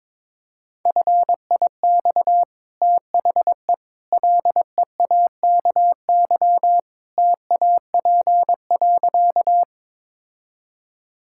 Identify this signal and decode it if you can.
Morse code, heard as a keyed tone, about 22 wpm, 700 Hz. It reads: FIX THE LEAKY TAP.
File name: practice / 22wpm / Easy Sentences with Hints - 22wpm